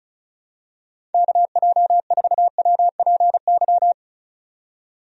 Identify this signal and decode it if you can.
Morse code, heard as a keyed tone, about 34 wpm, 700 Hz. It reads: KJ4WPY
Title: USA Call Signs - 6 Characters - 35wpm